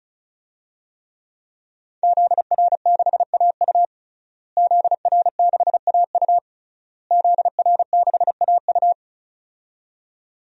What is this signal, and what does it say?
Morse code, heard as a keyed tone, about 36 wpm, 700 Hz. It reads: ZR6AU ZR6AU ZR6AU
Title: DXCC-100 Call Signs - 5 Characters - Repeated 3x - 35wpm